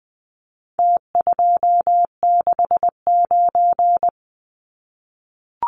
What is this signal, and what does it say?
Morse code, heard as a keyed tone, about 20 wpm, 700 Hz. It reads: T269
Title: Numbers - Contest Style - 4-Digit 20wpm